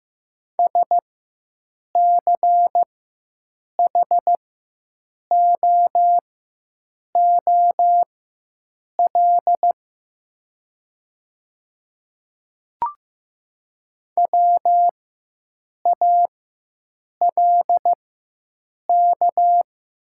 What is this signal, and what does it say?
Morse code, heard as a keyed tone, about 15 wpm, spaced as for 7 wpm, 700 Hz. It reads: SCHOOL WALK
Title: 4x Character Spacing - Top 500 Words - 15wpm